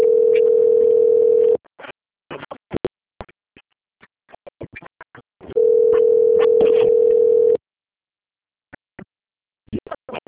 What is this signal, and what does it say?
Ringback tone (North America), 440 + 480 Hz.